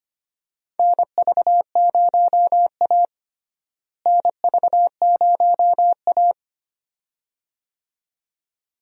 Morse code, heard as a keyed tone, about 25 wpm, 700 Hz. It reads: NV0A NV0A
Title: USA Call Signs - 4 Characters - Repeated 2x - 25wpm